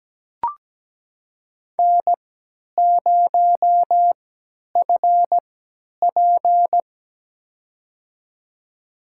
Morse code, heard as a keyed tone, about 17 wpm, spaced as for 10 wpm, 700 Hz. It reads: N0FP
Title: USA Call Signs - 4 Characters - 3x Character Spacing - 17wpm